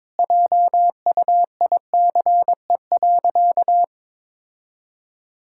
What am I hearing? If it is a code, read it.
Morse code, heard as a keyed tone, about 22 wpm, 700 Hz. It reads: JUICE.
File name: practice / 22wpm / Easy Sentences with Hints - 22wpm